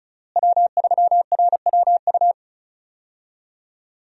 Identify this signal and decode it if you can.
Morse code, heard as a keyed tone, about 36 wpm, 700 Hz. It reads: W3RWU
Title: USA Call Signs - 5 Characters - Repeated 3x - 35wpm